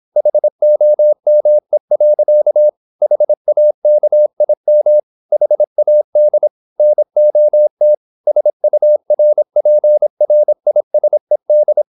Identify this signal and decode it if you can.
Morse code, heard as a keyed tone, about 26 wpm, 600 Hz. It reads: HOME. HAKIM HAD NOT SURPRISED
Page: Never 26wpm | CW med Gnister